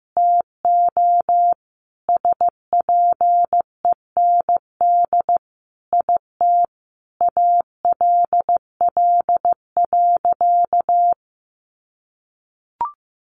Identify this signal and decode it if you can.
Morse code, heard as a keyed tone, about 15 wpm, 700 Hz. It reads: TO SPEND IT ALL.
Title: Sentences from Top 800 Words 15wpm